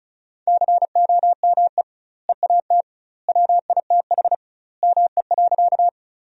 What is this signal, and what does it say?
Morse code, heard as a keyed tone, about 34 wpm, 700 Hz. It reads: COME EAT WITH ME.
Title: 2x Word-Spacing - Sentences from Top 300 Words - 35wpm